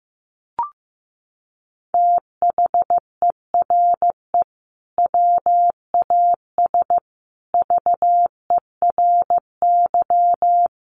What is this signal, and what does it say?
Morse code, heard as a keyed tone, about 15 wpm, 700 Hz. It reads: THERE WAS VERY
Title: Sentences from Top 1400 Words 15wpm